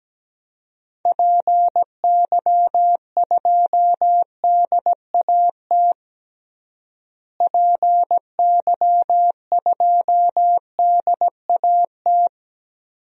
Morse code, heard as a keyed tone, about 17 wpm, 700 Hz. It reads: PY2DAT PY2DAT
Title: DXCC-100 Call Signs - 6 Characters - Repeated 3x - 17wpm